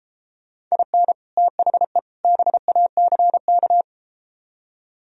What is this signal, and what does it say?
Morse code, heard as a keyed tone, about 33 wpm, 700 Hz. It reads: IN THE BACK